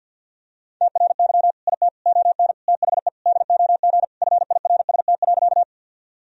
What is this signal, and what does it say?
Morse code, heard as a keyed tone, about 52 wpm, 700 Hz. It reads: TRY IT ON THE DOG FIRST.
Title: Sentences from Top 1000 Words - Review Part 2 - 50wpm